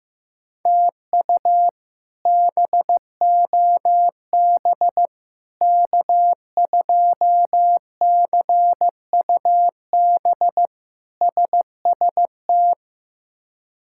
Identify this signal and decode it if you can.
Morse code, heard as a keyed tone, about 15 wpm, 700 Hz. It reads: TU BOB K2CUB SST
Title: K1USN SST QSO Practice - 15wpm